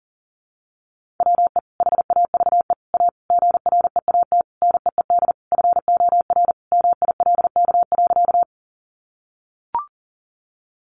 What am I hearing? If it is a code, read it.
Morse code, heard as a keyed tone, about 40 wpm, 700 Hz. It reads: WE HAVE A GREAT NEED FOR MILK.
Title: Sentences from Top 1500 Words 40wpm